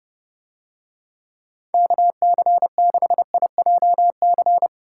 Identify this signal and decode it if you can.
Morse code, heard as a keyed tone, about 30 wpm, 700 Hz. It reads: KC6IJC